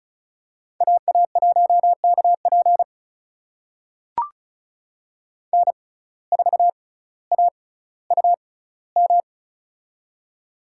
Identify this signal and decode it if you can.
Morse code, heard as a keyed tone, about 36 wpm, spaced as for 18 wpm, 700 Hz. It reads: AA1KP N4AUM
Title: USA Call Signs - 5 Characters - 6x Character Spacing - 35wpm